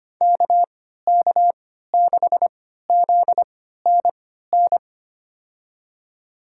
Morse code, heard as a keyed tone, about 25 wpm, spaced as for 14 wpm, 700 Hz. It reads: KK6ZNN
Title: USA Call Signs - 6 Characters - 3x Character Spacing - 25wpm